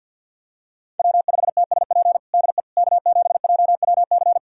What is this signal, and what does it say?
Morse code, heard as a keyed tone, about 51 wpm, 700 Hz. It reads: W5TSP DE K7JWC
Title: USA Callsign DE Callsign - 5 Characters - 50wpm